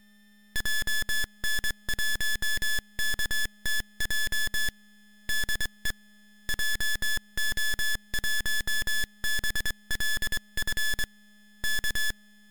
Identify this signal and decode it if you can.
Morse code, heard as a keyed tone, about 24 wpm, 1700 Hz, with noise. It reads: JN1KTJ DE JO1BLF K
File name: GokurakuChuukaTaisen-MorseCode.ogg / GokurakuChuukaTaisen-MorseCode.ogg